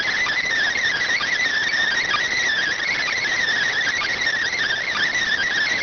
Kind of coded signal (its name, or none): Emergency Alert System (SAME) tones